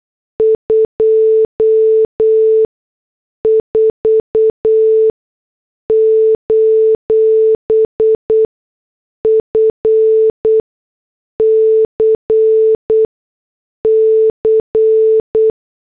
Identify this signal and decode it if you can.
Morse code, heard as a keyed tone, about 8 wpm, spaced as for 6 wpm, 440 Hz. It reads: 24: FCC
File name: morse / headlines